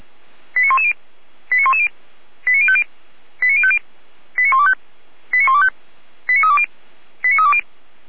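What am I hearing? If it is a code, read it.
Five-tone selective call (ZVEI), heard as quick runs of five tones, tones of about 70 ms. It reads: 89110 89110 89050 89050 89125 89125 89320 89320
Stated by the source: SDR-received ZVEI five-tone selective call.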